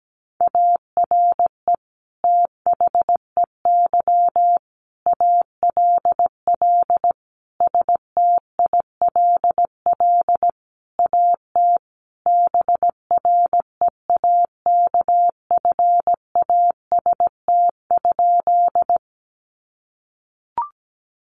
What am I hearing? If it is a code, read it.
Morse code, heard as a keyed tone, about 17 wpm, 700 Hz. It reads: ARE THEY ALL STILL AT BREAKFAST?
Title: Sentences from Top 1500 Words 17wpm